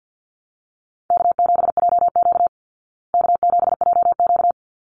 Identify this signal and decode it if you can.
Morse code, heard as a keyed tone, about 50 wpm, 700 Hz. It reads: K7JQ K7JQ
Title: Easy Call Signs Repeated 2x - 50wpm